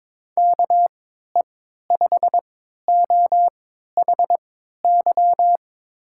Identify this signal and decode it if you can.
Morse code, heard as a keyed tone, about 22 wpm, spaced as for 13 wpm, 700 Hz. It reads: KE5OHY
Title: USA Call Signs - 6 Characters - 3x Character Spacing - 22wpm